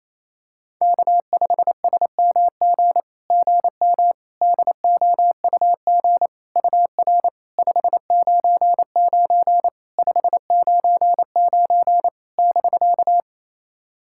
Morse code, heard as a keyed tone, about 28 wpm, 700 Hz. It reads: K5SMG GM DOUG UR 599 599 <BK>
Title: SOTA QSO Practice 28wpm